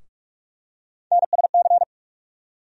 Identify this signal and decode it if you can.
Morse code, heard as a keyed tone, about 46 wpm, 700 Hz. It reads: NSC
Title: USA Call Sign Postfixes - 3 Letters 45wpm